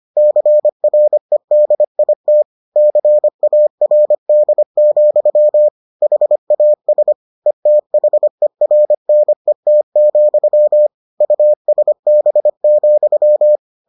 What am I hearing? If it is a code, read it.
Morse code, heard as a keyed tone, about 25 wpm, 600 Hz. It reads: CREDIT CARD, HAS ETHERNET, USB,